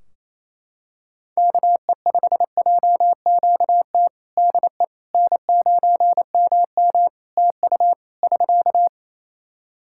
Morse code, heard as a keyed tone, about 28 wpm, 700 Hz. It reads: KE5JQT DE N9MM TU <SK>
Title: K1USN SST QSO Practice 28wpm